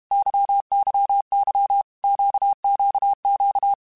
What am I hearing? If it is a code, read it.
Morse code, heard as a keyed tone, about 32 wpm, 800 Hz. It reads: YYY QQQ
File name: Groups@30 / Groups@30 mp3s / LFCYQ@30